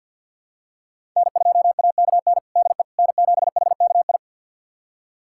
Morse code, heard as a keyed tone, about 46 wpm, 700 Hz. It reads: N2AKN DE N7HKI
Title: USA Callsign DE Callsign - 5 Characters - 50wpm